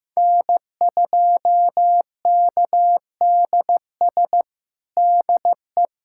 Morse code, heard as a keyed tone, about 15 wpm, 700 Hz. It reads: N2KDS DE
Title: USA Callsign DE Callsign - 5 Characters - 15wpm